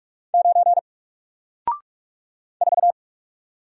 Morse code, heard as a keyed tone, about 44 wpm, 700 Hz. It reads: 9 4
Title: Single Letter-Number Alt 45wpm